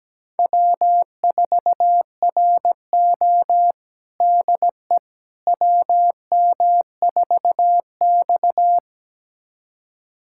Morse code, heard as a keyed tone, about 17 wpm, 700 Hz. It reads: W4RO DE WM4X
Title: USA Callsign DE Callsign - 4 Characters - 17wpm